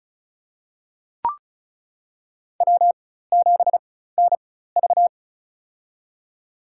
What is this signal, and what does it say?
Morse code, heard as a keyed tone, about 34 wpm, spaced as for 16 wpm, 700 Hz. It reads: W7NV
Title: USA Call Signs - 4 Characters - 4x Character Spacing - 35wpm